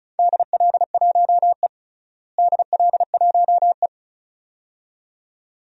Morse code, heard as a keyed tone, about 34 wpm, 700 Hz. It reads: DL1E DL1E
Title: DXCC-100 Call Signs - 4 Characters - Repeated 2x - 35wpm